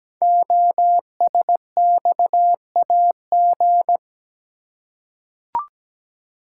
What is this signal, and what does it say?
Morse code, heard as a keyed tone, about 17 wpm, 700 Hz. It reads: OSXAG
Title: Code Groups - 5 Letters - 17wpm